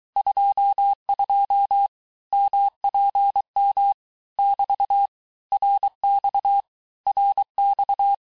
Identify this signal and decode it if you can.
Morse code, heard as a keyed tone, about 23 wpm, 800 Hz. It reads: 22 MPM <BT> RX RX
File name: cours-du-lundi files / cours-1588-texte-abreviations-vitesse-17mpm-puis-22mpm